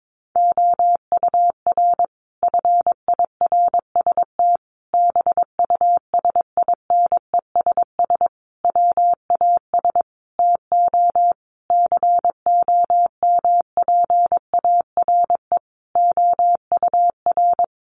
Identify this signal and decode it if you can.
Morse code, heard as a keyed tone, about 22 wpm, 700 Hz. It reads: OUR FIRST BUSINESS WAS TO COMPARE OUR
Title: Sentences from Top 1300 Words 22wpm